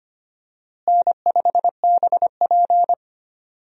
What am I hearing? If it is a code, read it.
Morse code, heard as a keyed tone, about 25 wpm, 700 Hz. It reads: N5BP
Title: USA Call Signs - 4 Characters - Repeated 3x - 25wpm